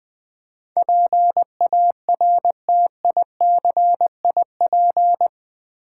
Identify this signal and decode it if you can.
Morse code, heard as a keyed tone, about 20 wpm, 700 Hz. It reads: PARTICIP
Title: Build A Word - 4 Letters At A Time - 11-Letter Words - 20wpm